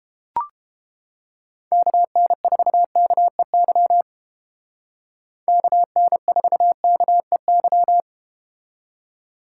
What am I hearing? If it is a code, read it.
Morse code, heard as a keyed tone, about 31 wpm, 700 Hz. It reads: KN4KEY KN4KEY